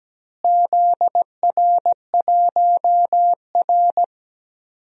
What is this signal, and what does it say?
Morse code, heard as a keyed tone, about 17 wpm, 700 Hz. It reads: ZR1R
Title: DXCC-100 Call Signs - 4 Characters - Repeated 3x - 17wpm